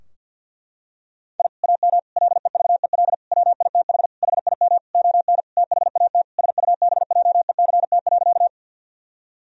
Morse code, heard as a keyed tone, about 50 wpm, 700 Hz. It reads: I AM LEVEL WITH HIM ON THAT SUBJECT.
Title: Sentences from Top 1000 Words - Review Part 4 - 50wpm